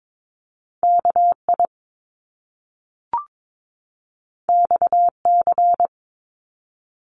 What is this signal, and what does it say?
Morse code, heard as a keyed tone, about 22 wpm, 700 Hz. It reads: KI XC
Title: Top Two Letter Combinations in Words 22wpm